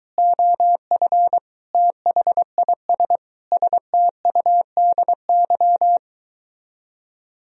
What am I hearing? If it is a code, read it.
Morse code, heard as a keyed tone, about 23 wpm, 700 Hz. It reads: OF THIS STUDY